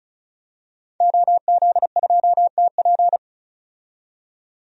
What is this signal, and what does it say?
Morse code, heard as a keyed tone, about 36 wpm, 700 Hz. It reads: OZ2TP
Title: DXCC-100 Call Signs - 5 Characters - 35wpm